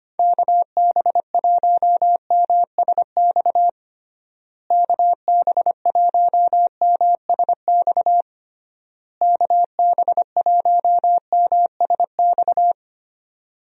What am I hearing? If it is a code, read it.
Morse code, heard as a keyed tone, about 25 wpm, 700 Hz. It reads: KB1MSX KB1MSX KB1MSX
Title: USA Call Signs - 6 Characters - Repeated 3x - 25wpm